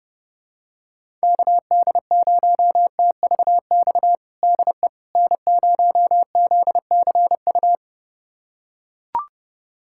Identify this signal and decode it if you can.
Morse code, heard as a keyed tone, about 32 wpm, 700 Hz. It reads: KD0TVX DE N0ZCU
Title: USA Callsign DE Callsign 30wpm